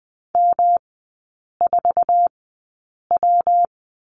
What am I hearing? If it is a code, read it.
Morse code, heard as a keyed tone, about 20 wpm, spaced as for 8 wpm, 690 Hz. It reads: M4W
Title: 145 Lesson 24e - Letter y - Three Characters 20wpm